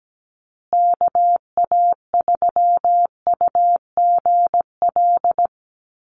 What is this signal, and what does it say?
Morse code, heard as a keyed tone, about 17 wpm, 700 Hz. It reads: KA3UGL
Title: USA Call Signs Repeated 2x 17wpm